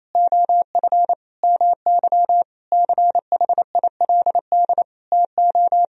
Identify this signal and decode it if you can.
Morse code, heard as a keyed tone, about 27 wpm, 700 Hz. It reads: OF MY CHILD TO
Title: Sentences from Top 1000 Words - Review Part 3 - 28wpm